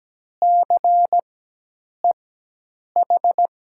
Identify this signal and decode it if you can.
Morse code, heard as a keyed tone, about 17 wpm, spaced as for 8 wpm, 700 Hz. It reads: CEH
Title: USA Call Signs - 6 Characters - 4x Character Spacing - 17wpm